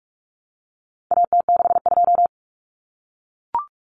Morse code, heard as a keyed tone, about 44 wpm, 700 Hz. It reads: AT62
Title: Common Cut Numbers - 4-Digit 45wpm